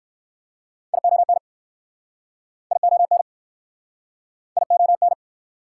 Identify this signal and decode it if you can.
Morse code, heard as a keyed tone, about 56 wpm, 700 Hz. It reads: ION ION ION
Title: Speed-Racing Prefixes and Suffixes 45wpm